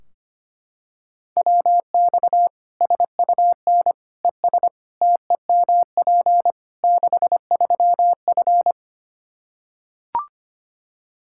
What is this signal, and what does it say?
Morse code, heard as a keyed tone, about 25 wpm, 700 Hz. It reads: WX SUN ES TEMP 63F
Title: Practice QSO - Exchange 2 - Info - 25wpm